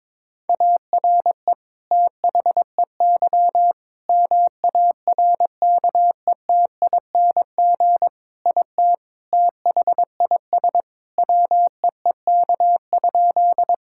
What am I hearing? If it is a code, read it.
Morse code, heard as a keyed tone, about 21 wpm, 700 Hz. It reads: ARE THEY MARKETING IT THIS WEEK?
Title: Sentences from Top 1000 Words - Review Part 5 - 22wpm